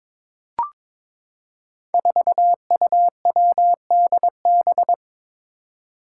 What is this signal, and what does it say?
Morse code, heard as a keyed tone, about 22 wpm, 700 Hz. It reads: 4UWDB
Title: Five Challenging Letter-Number Combinations 22wpm